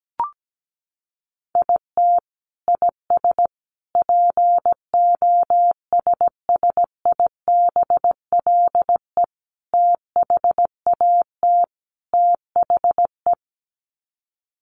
Morse code, heard as a keyed tone, about 17 wpm, 700 Hz. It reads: IT IS POSSIBLE THAT THE
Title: Sets of 5 Words Encore2 17wpm